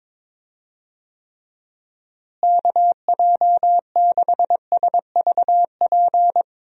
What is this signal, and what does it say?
Morse code, heard as a keyed tone, about 22 wpm, 700 Hz. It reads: KJ6SVP
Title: USA Call Signs - 6 Characters - Repeated 2x - 22wpm